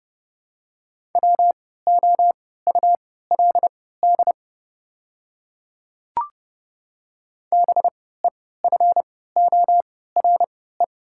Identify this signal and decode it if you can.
Morse code, heard as a keyed tone, about 30 wpm, spaced as for 17 wpm, 700 Hz. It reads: WOULD BEFORE
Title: 3x Character Spacing - Top 200 Words - 30wpm